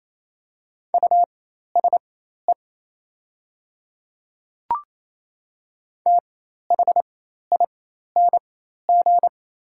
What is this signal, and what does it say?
Morse code, heard as a keyed tone, about 28 wpm, spaced as for 13 wpm, 700 Hz. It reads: USE THING
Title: 4x Character Spacing - Top 100 Words - 28wpm